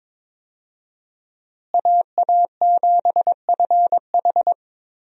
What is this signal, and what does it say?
Morse code, heard as a keyed tone, about 22 wpm, 700 Hz. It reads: AA7FH
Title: USA Call Signs - 5 Characters - Repeated 3x - 22wpm